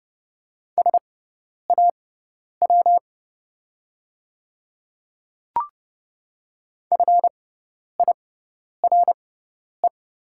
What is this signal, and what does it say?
Morse code, heard as a keyed tone, about 30 wpm, spaced as for 10 wpm, 690 Hz. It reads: SAW FIRE
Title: 6x Character Spacing - Top 500 Words - 30wpm